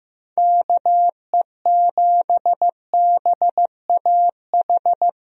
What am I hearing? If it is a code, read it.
Morse code, heard as a keyed tone, about 15 wpm, 700 Hz. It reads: KE7BAH
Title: USA Call Signs - 6 Characters - Repeated 2x - 15wpm